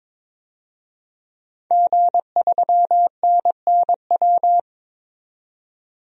Morse code, heard as a keyed tone, about 22 wpm, 700 Hz. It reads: G3NNW
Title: DXCC-100 Call Signs - 5 Characters - 22wpm